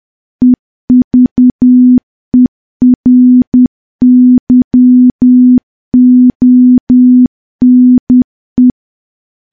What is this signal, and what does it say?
Morse code, heard as a keyed tone, about 10 wpm, 250 Hz. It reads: EVERYONE
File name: morse / morse-training-10WPM